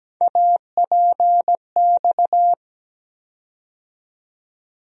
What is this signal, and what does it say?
Morse code, heard as a keyed tone, about 17 wpm, 700 Hz. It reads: APX
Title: USA Call Sign Postfixes - 3 Letters 17wpm